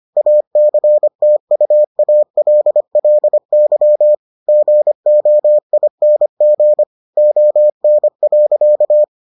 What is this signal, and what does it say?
Morse code, heard as a keyed tone, about 25 wpm, 600 Hz. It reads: ACTUALLY GOING ON.